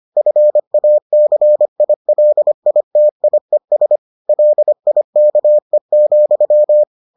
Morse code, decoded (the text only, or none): FACILITIES LIKE,